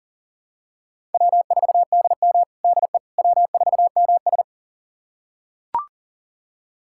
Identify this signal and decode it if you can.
Morse code, heard as a keyed tone, about 40 wpm, 700 Hz. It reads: W4DM DE W4MS
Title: USA Callsign DE Callsign - 4 Characters - 40wpm